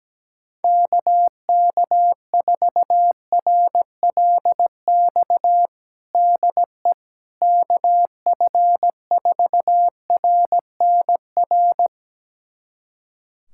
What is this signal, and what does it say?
Morse code, heard as a keyed tone, about 17 wpm, 700 Hz. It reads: KK4RLX DE KF4RNR
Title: USA Callsign DE Callsign - 6 Characters - 17wpm